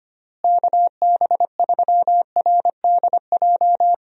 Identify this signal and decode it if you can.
Morse code, heard as a keyed tone, about 25 wpm, 700 Hz. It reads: KB3RDJ
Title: USA Call Signs - 6 Characters - Repeated 2x - 25wpm